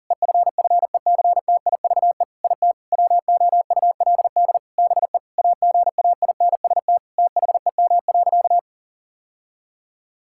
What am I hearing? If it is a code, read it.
Morse code, heard as a keyed tone, about 40 wpm, 700 Hz. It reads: EFFECTIVE IT WOULD BE AGAINST THEM.
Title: Sentences from Top 1000 Words - Review Part 6 - 40wpm